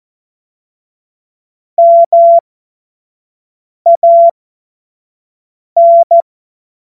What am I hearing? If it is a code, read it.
Morse code, heard as a keyed tone, about 14 wpm, spaced as for 5 wpm, 680 Hz. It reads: MAN
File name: Morsekurs / Witze / Witz 13. Juni 2023 25 F70